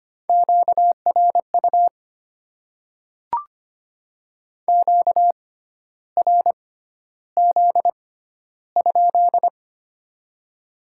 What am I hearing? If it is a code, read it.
Morse code, heard as a keyed tone, about 25 wpm, spaced as for 12 wpm, 700 Hz. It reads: QRU QRZ?
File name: Top 90 QSO Elements - 6x Character Spacing - 25wpm